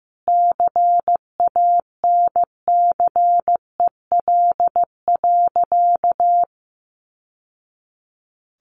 Morse code, heard as a keyed tone, about 15 wpm, 700 Hz. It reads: CANCEL.
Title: Sentences from Top 1200 Words 15wpm